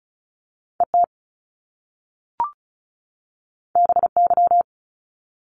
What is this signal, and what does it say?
Morse code, heard as a keyed tone, about 36 wpm, 700 Hz. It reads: ET BY
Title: Top Two Letter Combinations in Words 35wpm